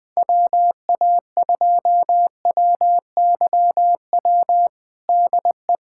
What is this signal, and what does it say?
Morse code, heard as a keyed tone, about 21 wpm, 700 Hz. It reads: WA2WYW DE
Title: USA Callsign DE Callsign - 6 Characters - 20wpm